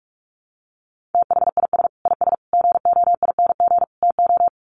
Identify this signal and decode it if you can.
Morse code, heard as a keyed tone, about 44 wpm, 700 Hz. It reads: THIS IS GOING TO
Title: Sets of 4 Words 45wpm